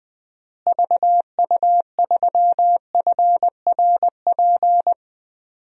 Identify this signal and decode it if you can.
Morse code, heard as a keyed tone, about 20 wpm, 700 Hz. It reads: VU3FRP
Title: DXCC-100 Call Signs - 6 Characters - Repeated 3x - 20wpm